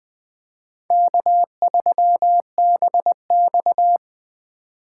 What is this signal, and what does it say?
Morse code, heard as a keyed tone, about 20 wpm, 700 Hz. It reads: K3BX
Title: Speed Racing - USA 1x2 Call Signs 17wpm